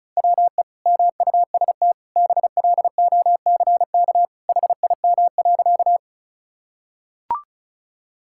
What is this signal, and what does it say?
Morse code, heard as a keyed tone, about 36 wpm, 700 Hz. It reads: WE MUST BLOCK HIM.
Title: Sentences from Top 1600 Words 35wpm